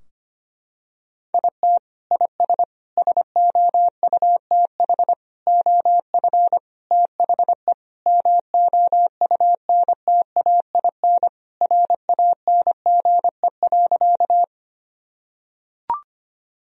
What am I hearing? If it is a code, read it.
Morse code, heard as a keyed tone, about 25 wpm, 700 Hz. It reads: IT IS SOUTH OF THE MOUNTAIN RANGE.
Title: Sentences from Top 1000 Words - Review Part 4 - 25wpm